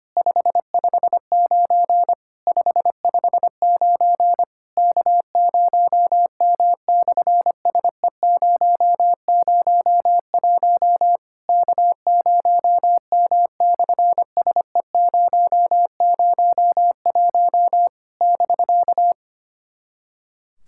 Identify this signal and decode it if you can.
Morse code, heard as a keyed tone, about 25 wpm, 700 Hz. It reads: 559 559 K0M/SE001 K0M/SE001 <BK>
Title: SOTA QSO Practice 25wpm